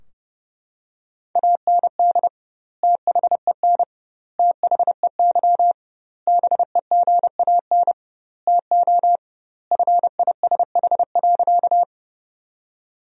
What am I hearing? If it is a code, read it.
Morse code, heard as a keyed tone, about 30 wpm, 700 Hz. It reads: AND THEN THEY BEGAN TO FISH.
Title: 2x Word-Spacing - Sentences from Top 300 Words - 30wpm